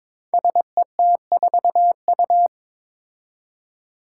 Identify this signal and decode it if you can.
Morse code, heard as a keyed tone, about 22 wpm, 700 Hz. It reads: SET4U
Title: Five Challenging Letter-Number Combinations 22wpm